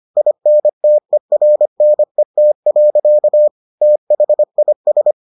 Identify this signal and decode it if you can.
Morse code, heard as a keyed tone, about 25 wpm, 600 Hz. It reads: INTERNET. THIS